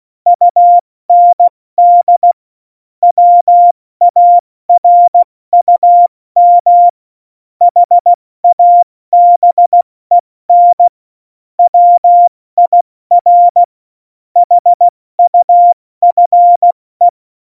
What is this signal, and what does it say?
Morse code, heard as a keyed tone, about 16 wpm, spaced as for 14 wpm, 700 Hz. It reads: UND WARUM HABEN WIR HUFE
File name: Morsekurs / Witze / Witz 06. November 2018, 70 F80